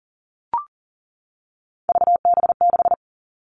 Morse code, heard as a keyed tone, about 40 wpm, 700 Hz. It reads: VB6
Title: All Cut Numbers - 3-Digit 40wpm